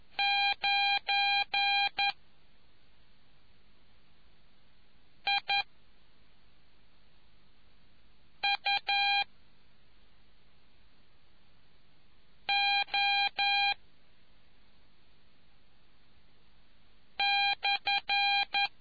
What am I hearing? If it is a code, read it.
Morse code, heard as a keyed tone, about 11 wpm, spaced as for 3 wpm, 2400 Hz. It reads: 9IUO/